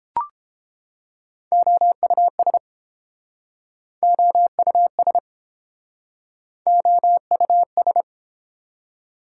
Morse code, heard as a keyed tone, about 30 wpm, 700 Hz. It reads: OUS OUS OUS